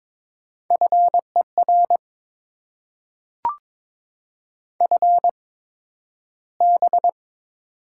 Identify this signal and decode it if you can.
Morse code, heard as a keyed tone, about 22 wpm, spaced as for 12 wpm, 700 Hz. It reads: FER FB
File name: Top 20 QSO Elements - 8x Character Spacing - 22wpm